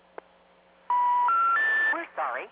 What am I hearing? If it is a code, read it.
Special information tone, with speech.